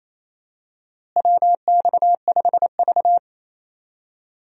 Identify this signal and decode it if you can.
Morse code, heard as a keyed tone, about 28 wpm, 700 Hz. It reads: WX5V
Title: USA Call Signs - 4 Characters - Repeated 2x - 28wpm